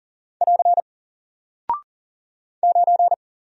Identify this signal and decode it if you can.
Morse code, heard as a keyed tone, about 40 wpm, 700 Hz. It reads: <AR> 9
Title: Single Character (Letter:Number:Punctuation:Prosign) Alt 40wpm